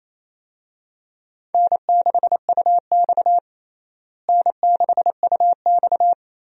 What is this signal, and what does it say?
Morse code, heard as a keyed tone, about 28 wpm, 700 Hz. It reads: N6UX N6UX
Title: USA Call Signs - 4 Characters - Repeated 2x - 28wpm